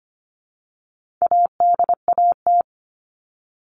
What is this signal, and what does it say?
Morse code, heard as a keyed tone, about 25 wpm, 700 Hz. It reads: ADAT
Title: All Cut Numbers - 4-Digit 25wpm